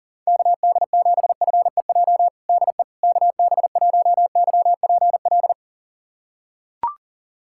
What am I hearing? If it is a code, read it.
Morse code, heard as a keyed tone, about 40 wpm, 700 Hz. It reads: KD7FEJ DE KB1YPL
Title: USA Callsign DE Callsign - 6 Characters - 40wpm